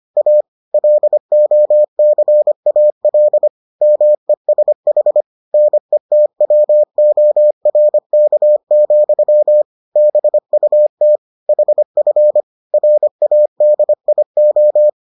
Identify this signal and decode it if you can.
Morse code, heard as a keyed tone, about 25 wpm, 600 Hz. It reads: A LOCAL MESH NETWORK, BUT HF RADIO